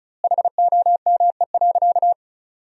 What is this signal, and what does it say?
Morse code, heard as a keyed tone, about 34 wpm, 700 Hz. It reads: HOME.
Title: 2x Word-Spacing - Sentences from Top 200 Words - 35wpm